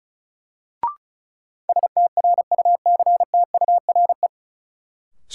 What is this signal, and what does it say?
Morse code, heard as a keyed tone, about 36 wpm, 700 Hz. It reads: STRUCTURE
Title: Sentences from Top 1000 Words - Review Part 6 - New Word Drill - 35wpm